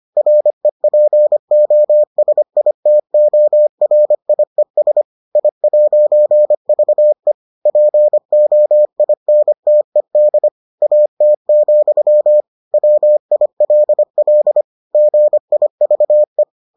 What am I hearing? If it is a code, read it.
Morse code, heard as a keyed tone, about 25 wpm, 600 Hz. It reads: REPOSITORIES I'VE POINTED AT, WILL GIVE